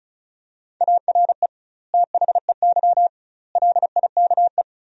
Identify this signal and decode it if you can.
Morse code, heard as a keyed tone, about 36 wpm, 700 Hz. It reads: ARE THEY LIKE
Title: 2x Word-Spacing - Sentences from Top 200 Words - 35wpm